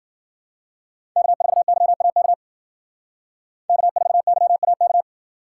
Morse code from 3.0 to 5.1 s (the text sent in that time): K4YAK